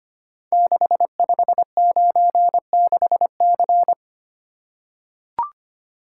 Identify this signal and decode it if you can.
Morse code, heard as a keyed tone, about 25 wpm, 700 Hz. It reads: 6596C
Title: Code Groups - 5 Letter:Numbers - 25wpm